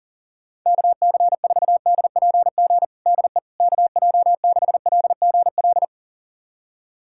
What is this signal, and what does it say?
Morse code, heard as a keyed tone, about 40 wpm, 700 Hz. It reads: KC4DPG DE KJ6LGL
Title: USA Callsign DE Callsign - 6 Characters - 40wpm